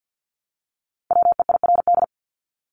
Morse code, heard as a keyed tone, about 50 wpm, 700 Hz. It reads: WEIRD
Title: Top 1251-1500 Words 50wpm